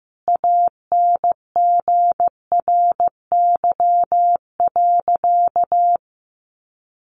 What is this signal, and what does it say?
Morse code, heard as a keyed tone, about 15 wpm, 700 Hz. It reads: ANGRY.
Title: Easy Phrases 15wpm